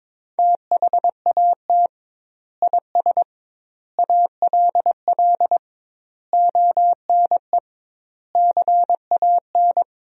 Morse code, heard as a keyed tone, about 22 wpm, 700 Hz. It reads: THAT IS ALL ONE CAN
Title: 2x Word-Spacing - Sentences from Top 200 Words - 22wpm